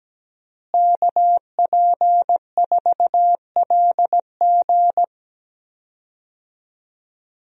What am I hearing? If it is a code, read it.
Morse code, heard as a keyed tone, about 17 wpm, 700 Hz. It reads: KP4LG
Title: DXCC-100 Call Signs - 5 Characters - Repeated 2x - 17wpm